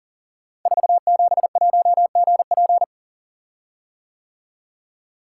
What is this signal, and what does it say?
Morse code, heard as a keyed tone, about 39 wpm, 700 Hz. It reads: 471GP
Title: Code Groups - 5 Letter:Numbers - 40wpm